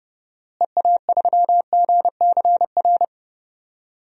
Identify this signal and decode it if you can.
Morse code, heard as a keyed tone, about 29 wpm, 700 Hz. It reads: EA3GCR
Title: DXCC-100 Call Signs - 6 Characters - Repeated 3x - 30wpm